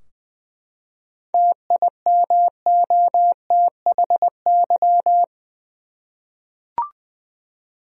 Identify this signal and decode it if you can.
Morse code, heard as a keyed tone, about 20 wpm, 700 Hz. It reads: TIMOTHY
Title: USA Common Names - 20wpm